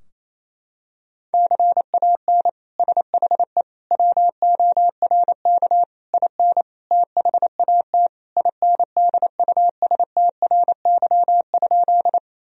Morse code, heard as a keyed tone, about 28 wpm, 700 Hz. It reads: CAN SHE WORK IN THAT INDUSTRY?
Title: Sentences from Top 1000 Words - Review Part 4 - 28wpm